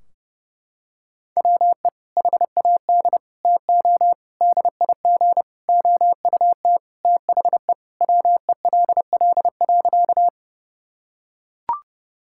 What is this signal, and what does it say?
Morse code, heard as a keyed tone, about 30 wpm, 700 Hz. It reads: WE HAD TO DIG OUT THE WELL.
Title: Sentences from Top 1600 Words 30wpm